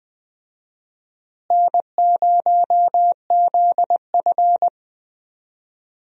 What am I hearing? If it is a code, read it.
Morse code, heard as a keyed tone, about 21 wpm, 700 Hz. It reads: N0ZF
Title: USA Call Signs - 4 Characters - Repeated 3x - 20wpm